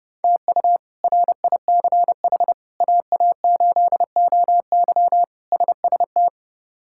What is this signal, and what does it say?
Morse code, heard as a keyed tone, about 30 wpm, 700 Hz. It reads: TU RICH AA8OY SST
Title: K1USN SST QSO Practice - 30wpm